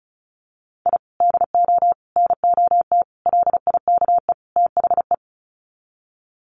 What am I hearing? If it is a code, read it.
Morse code, heard as a keyed tone, about 34 wpm, 700 Hz. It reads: I DO NOT LIKE THE
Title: Sets of 5 Words Encore 35wpm